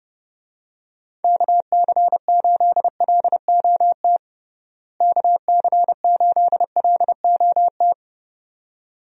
Morse code, heard as a keyed tone, about 30 wpm, 700 Hz. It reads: KC8LOT KC8LOT
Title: USA Call Signs - 6 Characters - Repeated 2x - 30wpm